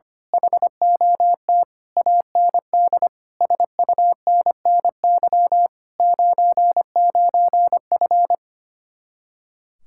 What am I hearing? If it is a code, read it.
Morse code, heard as a keyed tone, about 24 wpm, 700 Hz. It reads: HOT AND SUNNY 99F
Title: Speed-Racing Common QSO Phrases - 25wpm